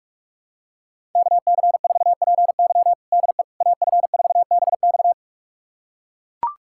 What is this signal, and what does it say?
Morse code, heard as a keyed tone, about 41 wpm, 700 Hz. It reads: KC4PY DE AF4BX
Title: USA Callsign DE Callsign - 5 Characters - 45wpm